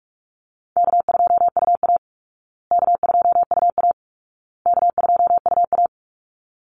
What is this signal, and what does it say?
Morse code, heard as a keyed tone, about 44 wpm, 700 Hz. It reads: K2UA K2UA K2UA
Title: Easy Call Signs Repeated 3x - 45wpm